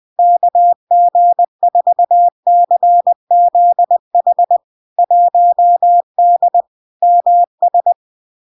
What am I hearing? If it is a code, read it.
Morse code, heard as a keyed tone, about 20 wpm, 700 Hz. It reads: KG4CZH 1D MS
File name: LARK-FD-2022-Log-in-CW-20wpm-700-Hz